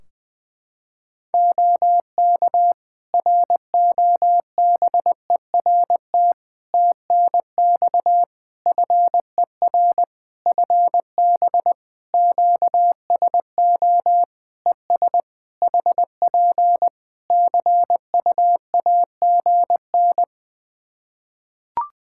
Morse code, heard as a keyed tone, about 20 wpm, 700 Hz. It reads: OK ROBERT TNX FER FB QSO ES HP CUAGN
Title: Practice QSO - Exchange 4 - Ending - 20wpm